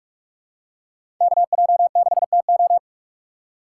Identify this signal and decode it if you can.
Morse code, heard as a keyed tone, about 41 wpm, 700 Hz. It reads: KJ6TO
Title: USA Call Signs - 5 Characters - 6x Character Spacing - 45wpm